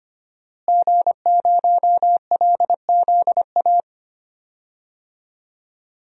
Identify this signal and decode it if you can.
Morse code, heard as a keyed tone, about 25 wpm, 700 Hz. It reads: G0LZA
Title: DXCC-100 Call Signs - 5 Characters - Repeated 2x - 25wpm